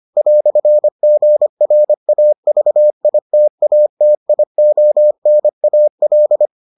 Morse code, heard as a keyed tone, about 25 wpm, 600 Hz. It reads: "GRAVITATIONAL